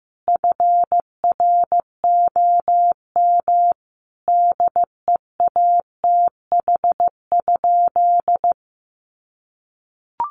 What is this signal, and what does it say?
Morse code, heard as a keyed tone, about 15 wpm, 700 Hz. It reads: FROM DEATH?
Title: Sentences from Top 1100 Words 15wpm